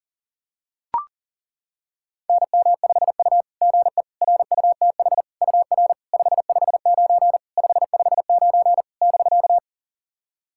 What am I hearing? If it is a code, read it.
Morse code, heard as a keyed tone, about 37 wpm, 690 Hz. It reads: NM5U GE RUTH UR 559 559 <BK>
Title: SOTA QSO Practice 40wpm